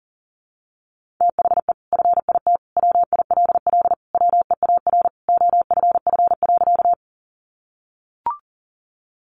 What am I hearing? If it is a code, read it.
Morse code, heard as a keyed tone, about 40 wpm, 700 Hz. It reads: THE FIT WILL WEAR OFF.
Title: Sentences from Top 1500 Words 40wpm